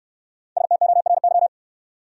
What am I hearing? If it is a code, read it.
Morse code, heard as a keyed tone, about 68 wpm, 700 Hz. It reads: STORY